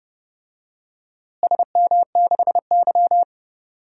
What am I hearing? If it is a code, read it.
Morse code, heard as a keyed tone, about 30 wpm, 700 Hz. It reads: SM6Y
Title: DXCC-100 Call Signs - 4 Characters - Repeated 3x - 30wpm